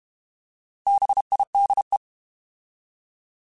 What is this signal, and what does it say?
Morse code, heard as a keyed tone, about 32 wpm, 800 Hz. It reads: BIDE
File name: Alphabetical / Alphabetical mp3s / Bs @30